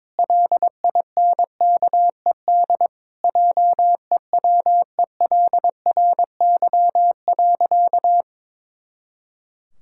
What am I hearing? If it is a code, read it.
Morse code, heard as a keyed tone, about 22 wpm, 700 Hz. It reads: LINKED JEWELRY.
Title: Pangram Challenge v2 - 22wpm